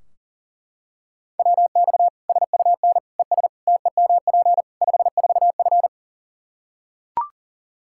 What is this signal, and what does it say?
Morse code, heard as a keyed tone, about 40 wpm, 700 Hz. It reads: WX SUN ES TEMP 54F
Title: Practice QSO - Exchange 2 - Info - 40wpm